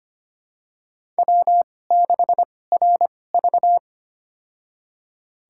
Morse code, heard as a keyed tone, about 25 wpm, spaced as for 18 wpm, 700 Hz. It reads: W6RV
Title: USA Call Signs - 4 Characters - 2x Character Spacing - 25wpm